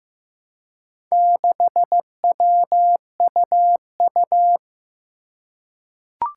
Morse code, heard as a keyed tone, about 15 wpm, 700 Hz. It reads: 6WUU
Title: Four Challenging Letter-Number Combinations 15wpm